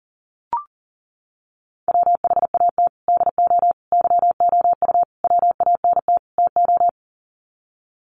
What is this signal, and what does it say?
Morse code, heard as a keyed tone, about 40 wpm, 700 Hz. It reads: WHAT DO YOU WANT TO
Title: Sets of 5 Words 40wpm